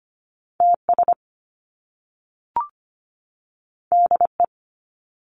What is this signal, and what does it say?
Morse code, heard as a keyed tone, about 25 wpm, 700 Hz. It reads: TS DE
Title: Top Two Letter Combinations in Words 25wpm